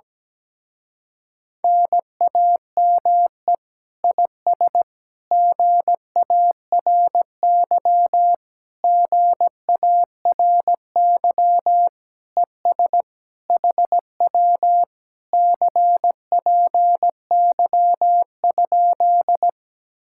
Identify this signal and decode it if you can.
Morse code, heard as a keyed tone, about 17 wpm, 700 Hz. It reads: NAME IS GARY GARY ES HW CPY?
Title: Speed-Racing Common QSO Phrases - 17wpm